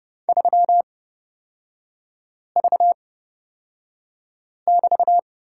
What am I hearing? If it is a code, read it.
Morse code, heard as a keyed tone, about 30 wpm, spaced as for 5 wpm, 700 Hz. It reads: 3V<BT>
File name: Lesson 36e - BT - Three Characters 30wpm